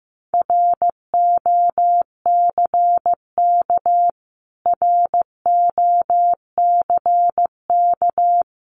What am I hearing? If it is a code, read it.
Morse code, heard as a keyed tone, about 15 wpm, 700 Hz. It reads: ROCK ROCK
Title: Top 500 Words Repeated 15wpm